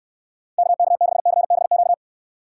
Morse code, heard as a keyed tone, about 69 wpm, 700 Hz. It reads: KK6CBQ